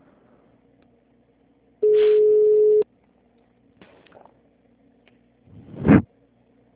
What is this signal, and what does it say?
Ringback tone (Europe, 425 Hz).